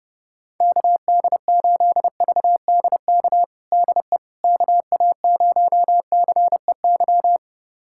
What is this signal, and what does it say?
Morse code, heard as a keyed tone, about 32 wpm, 700 Hz. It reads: KD8VDK DE KA0CEY
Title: USA Callsign DE Callsign 30wpm